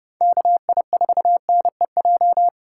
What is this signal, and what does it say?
Morse code, heard as a keyed tone, about 29 wpm, 700 Hz. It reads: KI4NEJ